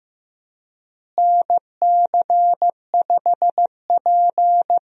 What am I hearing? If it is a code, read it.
Morse code, heard as a keyed tone, about 15 wpm, 700 Hz. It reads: NC5P
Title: USA Call Signs - 4 Characters - Repeated 2x - 15wpm